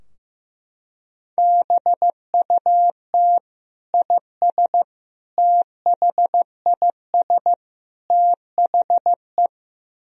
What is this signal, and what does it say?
Morse code, heard as a keyed tone, about 16 wpm, 700 Hz. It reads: BUT IS THIS THE
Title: Sentences from Top 1000 Words - Review Part 3 - 15wpm